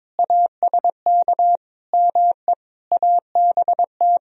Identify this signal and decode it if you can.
Morse code, heard as a keyed tone, about 22 wpm, 700 Hz. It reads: ASK ME ABT
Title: Sentences from Top 200 Words and Abbreviations 22wpm